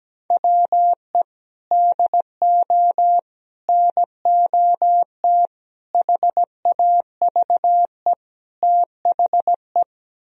Morse code, heard as a keyed tone, about 17 wpm, 700 Hz. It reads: WE DO NOT HAVE THE
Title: Speed-Racing - Sets of 5 Words 17wpm